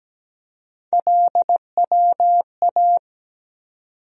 Morse code, heard as a keyed tone, about 17 wpm, 700 Hz. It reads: LWA
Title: USA Call Sign Postfixes - 3 Letters 17wpm